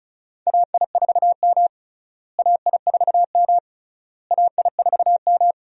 Morse code, heard as a keyed tone, about 36 wpm, 700 Hz. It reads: AI4M AI4M AI4M
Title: USA Call Signs - 4 Characters - Repeated 3x - 35wpm